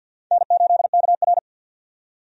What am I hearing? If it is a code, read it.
Morse code, heard as a keyed tone, about 50 wpm, 700 Hz. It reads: N8KR
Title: USA Call Signs - 4 Characters - 50wpm